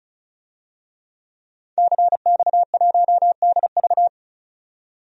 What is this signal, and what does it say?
Morse code, heard as a keyed tone, about 34 wpm, 700 Hz. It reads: CX1DV
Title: DXCC-100 Call Signs - 5 Characters - 35wpm